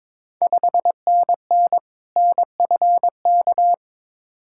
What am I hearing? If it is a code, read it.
Morse code, heard as a keyed tone, about 22 wpm, 700 Hz. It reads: 5NN NFK
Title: Ontario QSO Party Exchange 22wpm